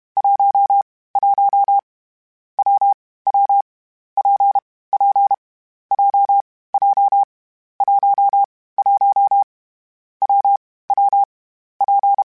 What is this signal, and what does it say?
Morse code, heard as a keyed tone, about 32 wpm, 800 Hz. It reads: 1 1 W W P P J J 1 1 W W P
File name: Variable Speed / Variable Speed-mp3s / WPJ WPJ1 sp @30